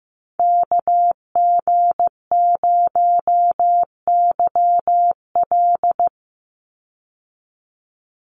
Morse code, heard as a keyed tone, about 15 wpm, 700 Hz. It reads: KG0YL
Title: Call Signs Repeated 2x - 15wpm